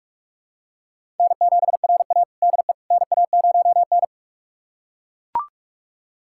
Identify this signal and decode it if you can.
Morse code, heard as a keyed tone, about 41 wpm, 700 Hz. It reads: N7RA DE NA0N
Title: USA Callsign DE Callsign - 4 Characters - 45wpm